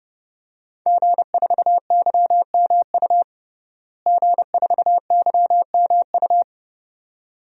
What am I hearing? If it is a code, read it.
Morse code, heard as a keyed tone, about 30 wpm, 700 Hz. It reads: G4YMU G4YMU
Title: DXCC-100 Call Signs - 5 Characters - Repeated 2x - 30wpm